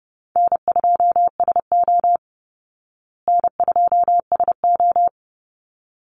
Morse code, heard as a keyed tone, about 30 wpm, 700 Hz. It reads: N2SO N2SO
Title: Easy Call Signs Repeated 2x - 30wpm